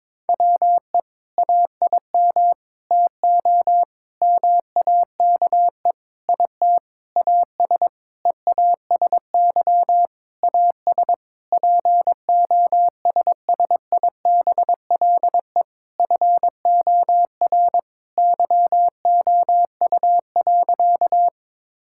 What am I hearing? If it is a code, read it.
Morse code, heard as a keyed tone, about 22 wpm, 700 Hz. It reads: WE AIM TO MAKE IT AS EASY AS POSSIBLE FOR YOU.
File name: Sentences with 7-Word Phrases - 22wpm